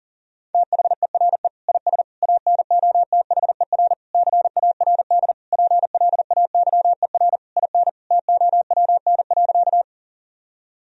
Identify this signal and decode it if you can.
Morse code, heard as a keyed tone, about 40 wpm, 700 Hz. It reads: THERE IS ANOTHER CARD PLAYER IN TOWN.
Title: Sentences from Top 1000 Words - Review Part 4 - 40wpm